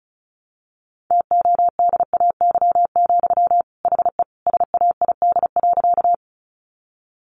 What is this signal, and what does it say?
Morse code, heard as a keyed tone, about 34 wpm, 700 Hz. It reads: TODAY, HE SAID.
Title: Sentences from Top 600 Words 35wpm